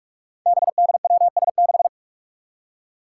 Morse code, heard as a keyed tone, about 46 wpm, 700 Hz. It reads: BDWS6
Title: Five Challenging Letter-Number Combinations 45wpm